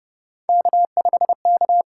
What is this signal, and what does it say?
Morse code, heard as a keyed tone, about 30 wpm, 700 Hz. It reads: K5K
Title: USA Call Signs - 5 Characters - 6x Character Spacing - 30wpm